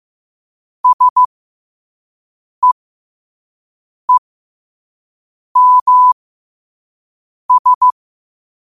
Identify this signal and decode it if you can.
Morse code, heard as a keyed tone, about 15 wpm, spaced as for 5 wpm, 1000 Hz. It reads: SEEMS
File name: qotdfiles / 5wpm